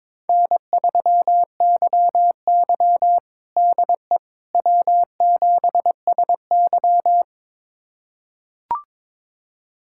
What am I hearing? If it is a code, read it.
Morse code, heard as a keyed tone, about 22 wpm, 700 Hz. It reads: N3YY DE W7SY
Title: USA Callsign DE Callsign - 4 Characters - 22wpm